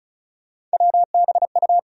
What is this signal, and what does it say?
Morse code, heard as a keyed tone, about 36 wpm, 700 Hz. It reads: WBU
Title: USA Call Sign Postfixes - 3 Letters 35wpm